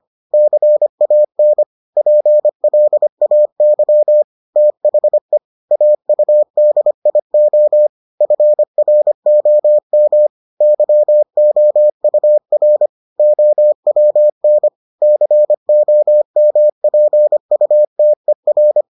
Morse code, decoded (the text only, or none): CAN PLAY THE AUDIO FROM YOUR OWN COMPUTER